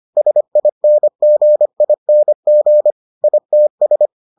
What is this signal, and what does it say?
Morse code, heard as a keyed tone, about 25 wpm, 600 Hz. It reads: SINGING ITS